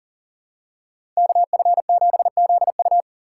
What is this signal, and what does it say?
Morse code, heard as a keyed tone, about 40 wpm, 700 Hz. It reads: KF7ZU
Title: USA Call Signs - 4x Character Spacing - 40wpm